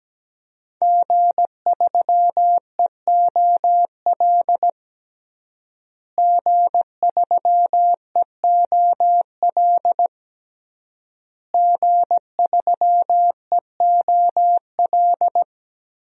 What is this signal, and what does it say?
Morse code, heard as a keyed tone, about 17 wpm, 700 Hz. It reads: G3EOL G3EOL G3EOL
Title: DXCC-100 Call Signs - 5 Characters - Repeated 3x - 17wpm